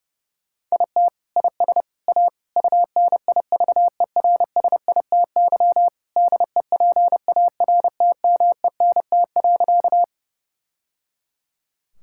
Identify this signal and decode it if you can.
Morse code, heard as a keyed tone, about 30 wpm, 700 Hz. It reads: IT IS A UNIVERSITY DEPARTMENT.
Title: Sentences from Top 1000 Words - Review Part 5 - 30wpm